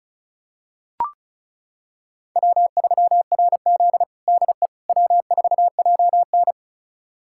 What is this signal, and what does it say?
Morse code, heard as a keyed tone, about 36 wpm, 700 Hz. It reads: W3RZ DE W4JN
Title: USA Callsign DE Callsign - 4 Characters - 35wpm